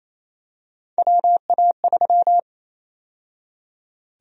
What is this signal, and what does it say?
Morse code, heard as a keyed tone, about 28 wpm, 700 Hz. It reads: WA3
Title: USA Call Sign Prefixes - 28wpm